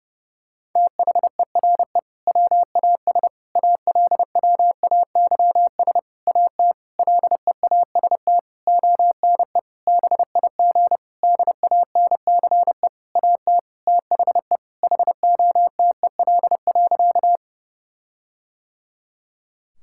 Morse code, heard as a keyed tone, about 30 wpm, 700 Hz. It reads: THERE WAS ALWAYS AT LEAST ONE BIG DANCE AT THE HOTEL.
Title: Sentences from Top 1600 Words 30wpm